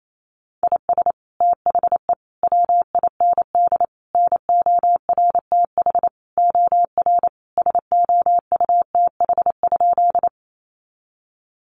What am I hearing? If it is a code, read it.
Morse code, heard as a keyed tone, about 28 wpm, 700 Hz. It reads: IS THE WIND NORTH OR SOUTH?
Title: Sentences from Top 400 Words 28wpm